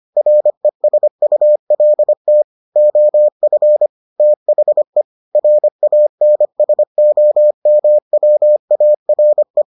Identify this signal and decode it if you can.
Morse code, heard as a keyed tone, about 25 wpm, 600 Hz. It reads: RESULT OF THE RANSOMWARE